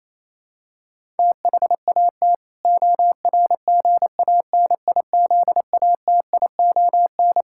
Morse code, heard as a keyed tone, about 28 wpm, 700 Hz. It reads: THAT ORGANIZATION
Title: Sentences from Top 1000 Words - Review Part 6 - 28wpm